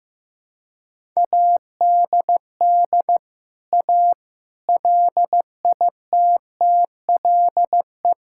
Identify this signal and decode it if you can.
Morse code, heard as a keyed tone, about 16 wpm, 700 Hz. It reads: ADD A LITTLE
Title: Sentences from Top 1000 Words - Review Part 5 - 15wpm